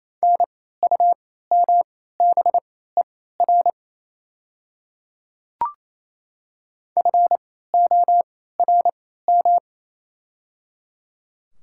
Morse code, heard as a keyed tone, about 28 wpm, spaced as for 16 wpm, 700 Hz. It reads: NUMBER FORM
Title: 3x Character Spacing - Top 300 Words - 28wpm